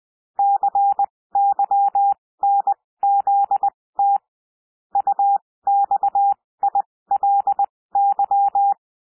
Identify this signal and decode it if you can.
Morse code, heard as a keyed tone, about 20 wpm, spaced as for 16 wpm, 800 Hz. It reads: CYNZT UXILY
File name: cwmp3 / L1 SOUND LETTERS 20 wpm